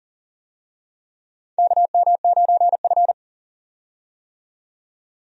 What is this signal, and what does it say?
Morse code, heard as a keyed tone, about 40 wpm, 700 Hz. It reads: KM9F
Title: USA Call Signs - 4 Characters - 40wpm